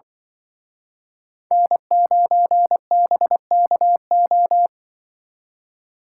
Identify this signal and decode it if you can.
Morse code, heard as a keyed tone, about 25 wpm, 700 Hz. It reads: N9BKO